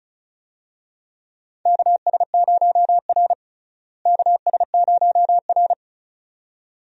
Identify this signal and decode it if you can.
Morse code, heard as a keyed tone, about 34 wpm, 700 Hz. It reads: KS0R KS0R
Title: USA Call Signs - 4 Characters - Repeated 2x - 35wpm